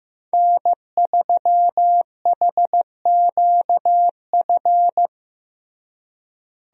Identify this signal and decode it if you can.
Morse code, heard as a keyed tone, about 15 wpm, 700 Hz. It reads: N3HQF
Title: USA Call Signs - 5 Characters - Repeated 2x - 15wpm